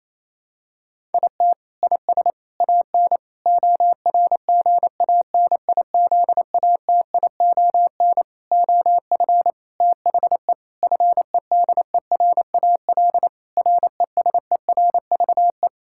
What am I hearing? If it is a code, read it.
Morse code, heard as a keyed tone, about 28 wpm, 700 Hz. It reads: IT IS AN ORGANIZATION OF THE FEDERAL RESERVE
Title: Sentences from Top 1000 Words - Review Part 6 - 28wpm